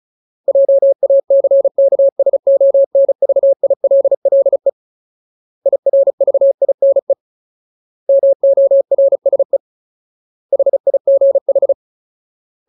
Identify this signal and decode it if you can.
Morse code, heard as a keyed tone, about 34 wpm, 550 Hz. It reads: JACKSONVILLE IRVINE MORSE HIGH
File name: Ultimate Word List / 35WPM Ultimate